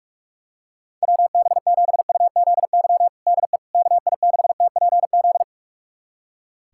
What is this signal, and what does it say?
Morse code, heard as a keyed tone, about 46 wpm, 700 Hz. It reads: WB7UZY DE KI6TPZ
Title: USA Callsign DE Callsign - 6 Characters - 45wpm